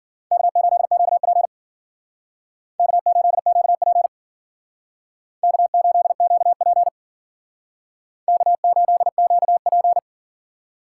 Morse code, heard as a keyed tone, about 51 wpm, 700 Hz. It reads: K8QP K8QP K8QP K8QP